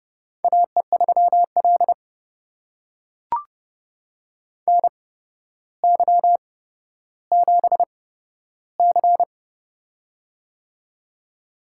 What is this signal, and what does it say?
Morse code, heard as a keyed tone, about 29 wpm, spaced as for 13 wpm, 700 Hz. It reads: AE3L NY7C
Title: USA Call Signs - 4 Characters - 8x Character Spacing - 30wpm